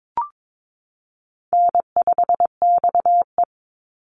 Morse code, heard as a keyed tone, about 22 wpm, 700 Hz. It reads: N5XE
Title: Easy Call Signs Repeated 2x - 22wpm